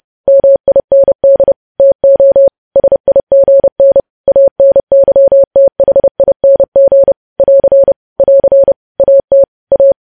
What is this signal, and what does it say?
Morse code, heard as a keyed tone, about 30 wpm, 550 Hz. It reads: MIND TO SIGN ANYTHING <AR> <AR> AT A
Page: news headlines in morse code at 30 wpm